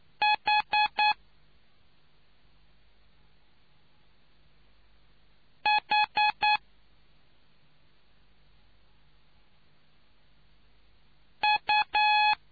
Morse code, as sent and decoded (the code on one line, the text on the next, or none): .... / .... / ..-
H H U